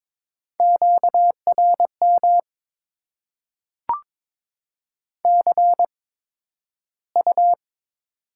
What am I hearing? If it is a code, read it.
Morse code, heard as a keyed tone, about 22 wpm, spaced as for 12 wpm, 700 Hz. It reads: QRM CU
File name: Top 90 QSO Elements - 8x Character Spacing - 22wpm